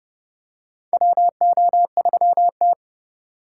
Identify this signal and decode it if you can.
Morse code, heard as a keyed tone, about 30 wpm, 700 Hz. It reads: WO3T
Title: USA Call Signs - 4 Characters - 4x Character Spacing - 30wpm